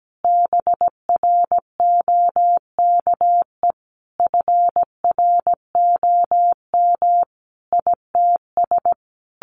Morse code, heard as a keyed tone, about 17 wpm, 700 Hz. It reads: BROKE FROM ITS
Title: A Sound Of Thunder 17wpm